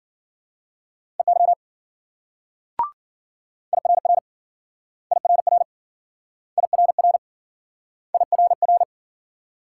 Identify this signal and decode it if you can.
Morse code, heard as a keyed tone, about 53 wpm, 700 Hz. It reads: EY IRR IRR IRR IRR